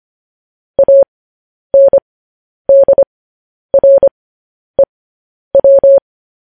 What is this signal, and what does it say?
Morse code, heard as a keyed tone, about 25 wpm, spaced as for 10 wpm, 550 Hz. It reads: ANDREW
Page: news headlines in morse code at 10 wpm